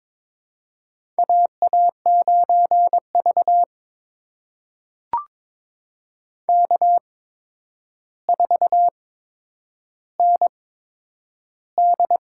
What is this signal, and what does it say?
Morse code, heard as a keyed tone, about 22 wpm, spaced as for 9 wpm, 700 Hz. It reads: AA9V K4ND
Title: USA Call Signs - 4 Characters - 8x Character Spacing - 22wpm